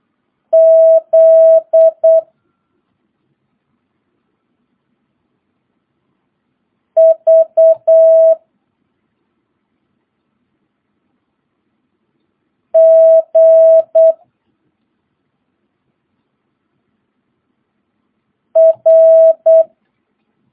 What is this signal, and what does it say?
Morse code, heard as a keyed tone, about 8 wpm, 650 Hz. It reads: Z V G R
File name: jh8cbh / 2022cw